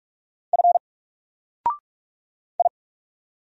Morse code, heard as a keyed tone, about 44 wpm, 700 Hz. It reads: F I
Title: Single Character (Letter:Number:Punctuation:Prosign) 45wpm